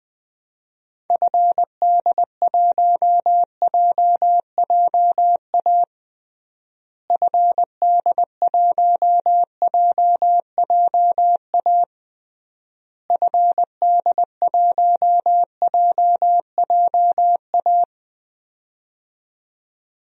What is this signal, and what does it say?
Morse code, heard as a keyed tone, about 21 wpm, 700 Hz. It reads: FD1JJA FD1JJA FD1JJA
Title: DXCC-100 Call Signs - 6 Characters - Repeated 3x - 20wpm